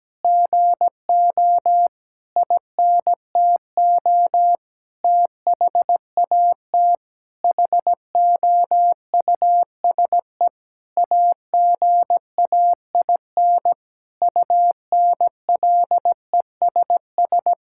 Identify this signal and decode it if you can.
Morse code, heard as a keyed tone, about 17 wpm, 700 Hz. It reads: GO INTO THAT HOUSE AGAIN UNLESS
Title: Sentences from Top 1000 Words - Review Part 6 - 17wpm